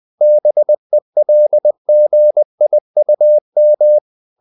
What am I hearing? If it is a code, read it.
Morse code, heard as a keyed tone, about 20 wpm, 600 Hz. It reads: BELGIUM
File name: CW / Practice / NationalCapitals20